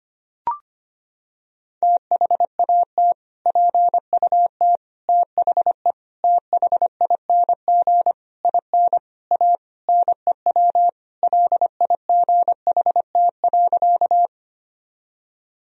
Morse code, heard as a keyed tone, about 25 wpm, 700 Hz. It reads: THAT PUT THE THING IN A NEW LIGHT.
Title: Sentences from Top 1000 Words - Review Part 1 - 25wpm